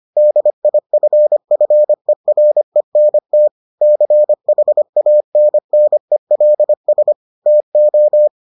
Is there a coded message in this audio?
Yes: Morse code from 0.2 to 8.3 s, heard as a keyed tone, about 25 wpm, 600 Hz. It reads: DIFFERENT CHANNELS TO